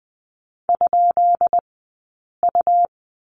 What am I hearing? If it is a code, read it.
Morse code, heard as a keyed tone, about 20 wpm, spaced as for 8 wpm, 700 Hz. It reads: ?U
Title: 197 Lesson 32d - Letter k - Two Characters 20wpm